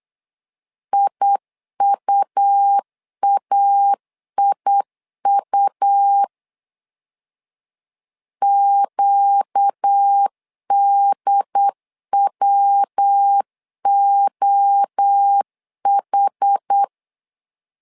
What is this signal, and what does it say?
Morse code, heard as a keyed tone, about 8 wpm, 800 Hz. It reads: IUAIU QDWOH
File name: 4 Q 8wpm